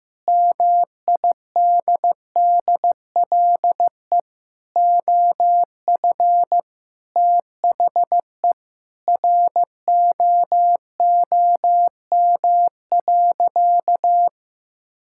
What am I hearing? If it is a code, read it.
Morse code, heard as a keyed tone, about 15 wpm, 700 Hz. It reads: MIDDLE OF THE ROOM.
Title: Sentences from Top 1000 Words - Review Part 6 - 15wpm